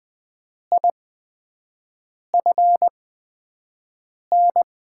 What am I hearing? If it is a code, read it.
Morse code, heard as a keyed tone, about 20 wpm, spaced as for 5 wpm, 700 Hz. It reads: IFN
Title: USA Call Signs - 6 Characters - 8x Character Spacing - 20wpm